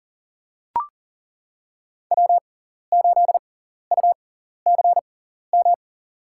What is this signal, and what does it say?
Morse code, heard as a keyed tone, about 40 wpm, spaced as for 14 wpm, 700 Hz. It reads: W8UCM
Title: USA Call Signs - 5 Characters - 6x Character Spacing - 40wpm